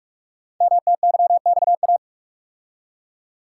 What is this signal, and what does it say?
Morse code, heard as a keyed tone, about 44 wpm, 700 Hz. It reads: MTYXA
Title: Code Groups - 5 Letters - 45wpm